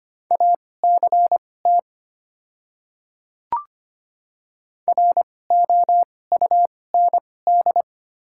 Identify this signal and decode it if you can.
Morse code, heard as a keyed tone, about 25 wpm, spaced as for 18 wpm, 690 Hz. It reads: ACT ROUND
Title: 2x Character Spacing - Top 300 Words - 25wpm